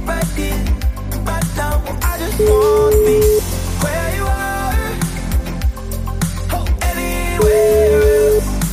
Ringback tone (Europe, 425 Hz), with noise.